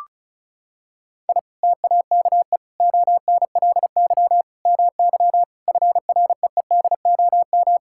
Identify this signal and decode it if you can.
Morse code, heard as a keyed tone, about 34 wpm, 700 Hz. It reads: I TAKE ONLY MY FREEDOM
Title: Sentences from Top 1000 Words - Review Part 5 - 35wpm